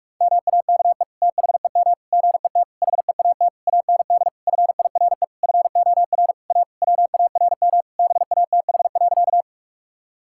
Morse code, heard as a keyed tone, about 46 wpm, 700 Hz. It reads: MAKE THEM GET HEAT AND FIRE FOR A WARM BATH.
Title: Sentences from Top 1600 Words 45wpm